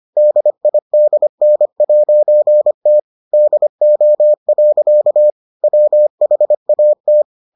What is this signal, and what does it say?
Morse code, heard as a keyed tone, about 25 wpm, 600 Hz. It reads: DIDN'T DO. WHAT